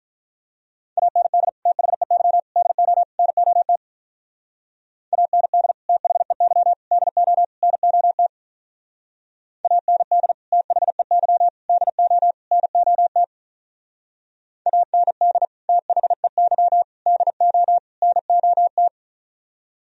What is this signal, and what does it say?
Morse code, heard as a keyed tone, about 43 wpm, 700 Hz. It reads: AND THEY DO NOT AND THEY DO NOT AND THEY DO NOT AND THEY DO NOT
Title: Speed-Racing - Sets of 4 Words 35wpm